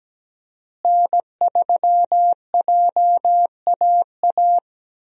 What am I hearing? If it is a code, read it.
Morse code, heard as a keyed tone, about 17 wpm, 700 Hz. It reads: N3JAA
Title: USA Call Signs - 5 Characters - Repeated 2x - 17wpm